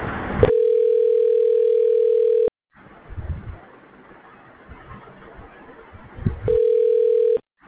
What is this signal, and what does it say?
Ringback tone (North America), 440 + 480 Hz.